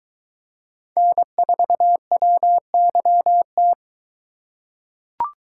Morse code, heard as a keyed tone, about 22 wpm, 700 Hz. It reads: N4WYT